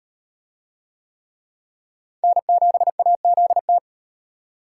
Morse code, heard as a keyed tone, about 35 wpm, 700 Hz. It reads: N7AZT